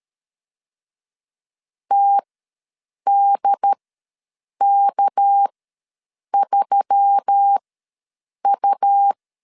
Morse code, heard as a keyed tone, about 13 wpm, spaced as for 7 wpm, 800 Hz. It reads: TDK3U
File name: LEZIONI / 6 PKX0 / 6 PK ws 12wpm 3116